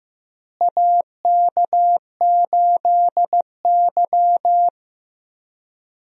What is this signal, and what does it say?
Morse code, heard as a keyed tone, about 15 wpm, 700 Hz. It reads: AK8Y
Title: USA Call Signs - 4 Characters - Repeated 2x - 15wpm